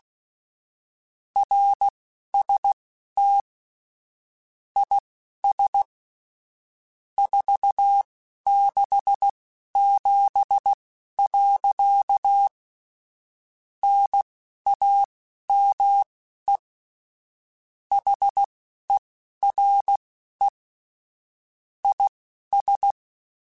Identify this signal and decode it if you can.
Morse code, heard as a keyed tone, about 16 wpm, spaced as for 11 wpm, 800 Hz. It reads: RST IS 467. NAME HERE IS
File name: QSOs / QSOs-mp3s / kc7haa@10